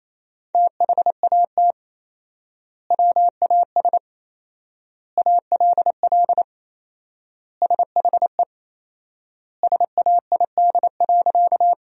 Morse code, heard as a keyed tone, about 28 wpm, 700 Hz. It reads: THAT WAS ALL SHE SAID.
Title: 4x Word-Spacing - Sentences from Top 100 Words - 28wpm